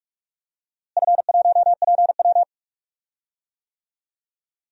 Morse code, heard as a keyed tone, about 45 wpm, 700 Hz. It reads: F1PW
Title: DXCC-100 Call Signs - 4 Characters - Repeated 2x - 45wpm